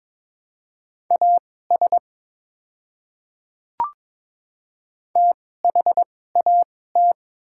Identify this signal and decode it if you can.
Morse code, heard as a keyed tone, about 21 wpm, spaced as for 16 wpm, 700 Hz. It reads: AS THAT
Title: 2x Character Spacing - Top 100 Words - 22wpm